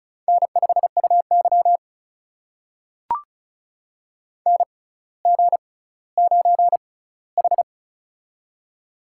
Morse code, heard as a keyed tone, about 36 wpm, spaced as for 18 wpm, 700 Hz. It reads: N5UY NG9H
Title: USA Call Signs - 4 Characters - 6x Character Spacing - 35wpm